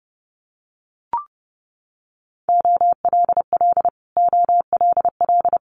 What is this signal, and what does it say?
Morse code, heard as a keyed tone, about 30 wpm, 700 Hz. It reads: OLL OLL
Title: Multi-letter Phonemes Repeated 3x 30wpm